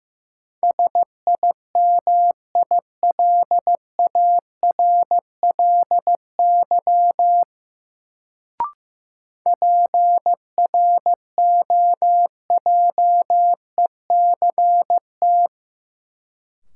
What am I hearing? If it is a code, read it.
Morse code, heard as a keyed tone, about 15 wpm, 700 Hz. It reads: SIMILARLY PROJECT
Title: Sentences from Top 1400 Words - New Word Drill Alt - 15wpm